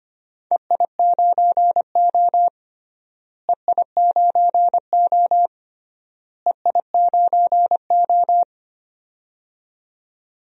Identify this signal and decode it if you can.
Morse code, heard as a keyed tone, about 25 wpm, 700 Hz. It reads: EI9O EI9O EI9O
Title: DXCC-100 Call Signs - 4 Characters - Repeated 3x - 25wpm